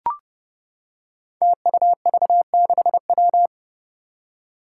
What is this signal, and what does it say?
Morse code, heard as a keyed tone, about 32 wpm, 700 Hz. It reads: TUV6W
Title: Five Challenging Letter-Number Combinations 30wpm